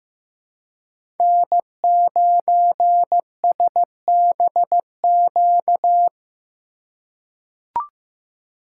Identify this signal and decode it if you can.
Morse code, heard as a keyed tone, about 15 wpm, 700 Hz. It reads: N9SBQ
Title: USA Call Signs - 4x Character Spacing - 15wpm